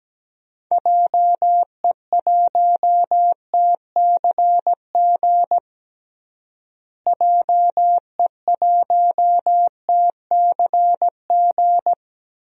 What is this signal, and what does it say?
Morse code, heard as a keyed tone, about 17 wpm, 700 Hz. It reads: JE1TCG JE1TCG
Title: DXCC-100 Call Signs - 6 Characters - Repeated 3x - 17wpm